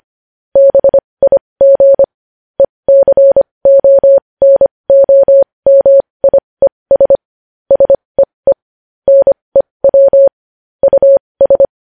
Morse code, heard as a keyed tone, about 25 wpm, spaced as for 20 wpm, 550 Hz. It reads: BIG ECONOMIES SEE NEW US
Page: news headlines in morse code at 20 wpm